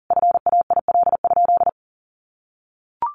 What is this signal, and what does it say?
Morse code, heard as a keyed tone, about 40 wpm, 700 Hz. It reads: FAIL?
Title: Sentences from Top 1400 Words 40wpm